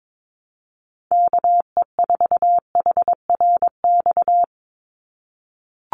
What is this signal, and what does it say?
Morse code, heard as a keyed tone, about 22 wpm, 700 Hz. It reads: KE4HRX
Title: USA Call Signs 22wpm